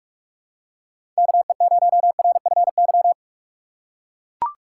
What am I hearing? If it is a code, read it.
Morse code, heard as a keyed tone, about 42 wpm, 700 Hz. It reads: KE0RFY
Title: USA Call Signs - 6 Characters - 4x Character Spacing - 45wpm